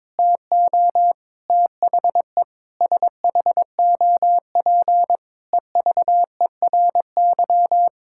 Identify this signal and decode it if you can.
Morse code, heard as a keyed tone, about 21 wpm, 700 Hz. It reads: TO THE SHOP EVERY
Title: Sentences from Top 1600 Words 22wpm